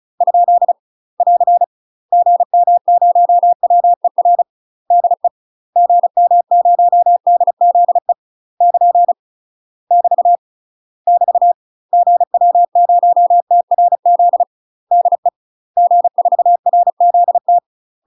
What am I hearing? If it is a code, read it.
Morse code, heard as a keyed tone, about 36 wpm, 700 Hz. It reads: ? <AR> GM0WER DE GM0DZE <KN> <BT> <BT> GW0TRZ DE G4RZT
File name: UK MASTER QSOs TRAINING / 35WPM / UK MASTER QSOs 0004